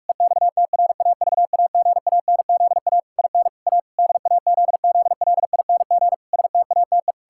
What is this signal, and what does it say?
Morse code, heard as a keyed tone, about 46 wpm, 700 Hz. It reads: EXTRAVAGANZA IN A DAZZLING STATE
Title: Pangram Challenge v2 - 45wpm